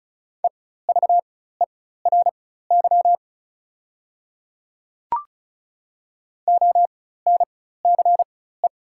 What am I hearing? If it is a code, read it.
Morse code, heard as a keyed tone, about 34 wpm, spaced as for 16 wpm, 690 Hz. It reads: EVERY ONCE
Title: 4x Character Spacing - Top 300 Words - 35wpm